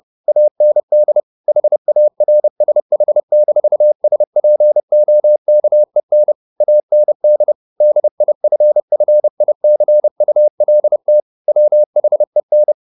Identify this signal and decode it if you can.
Morse code, heard as a keyed tone, about 30 wpm, 600 Hz. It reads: AND HARSH-SPOKEN AND DIFFICULT WHEN
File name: the old man and the sea / the old man and the sea 1 30wpm